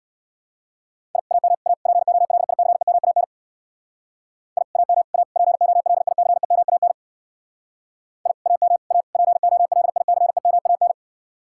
Morse code, heard as a keyed tone, about 65 wpm, 700 Hz. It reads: I AM A POLICEMAN I AM A POLICEMAN I AM A POLICEMAN